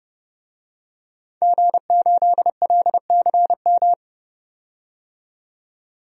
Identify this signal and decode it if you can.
Morse code, heard as a keyed tone, about 30 wpm, 700 Hz. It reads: G8LCM
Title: DXCC-100 Call Signs - 5 Characters - 30wpm